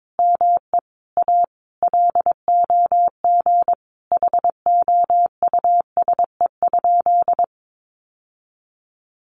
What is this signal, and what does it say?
Morse code, heard as a keyed tone, about 22 wpm, 700 Hz. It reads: ME A LOG HOUSE?
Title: Sentences from Top 1400 Words 22wpm